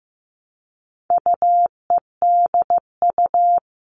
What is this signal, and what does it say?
Morse code, heard as a keyed tone, about 15 wpm, 700 Hz. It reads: UEDU
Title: All Cut Numbers - 4-Digit 15wpm